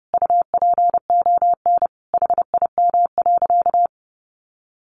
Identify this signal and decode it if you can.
Morse code, heard as a keyed tone, about 30 wpm, 700 Hz. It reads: UPON HIM.
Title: Sentences from Top 1100 Words 30wpm